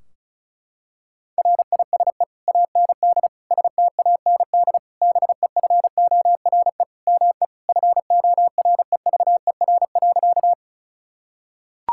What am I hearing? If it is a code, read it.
Morse code, heard as a keyed tone, about 34 wpm, 700 Hz. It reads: RISE AND STAND BEFORE ME FOREVER.
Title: Sentences from Top 1600 Words 35wpm